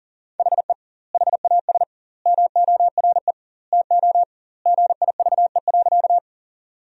Morse code, heard as a keyed tone, about 40 wpm, 700 Hz. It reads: HE HAS MORE TO GIVE.
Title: 2x Word-Spacing - Sentences from Top 200 Words - 40wpm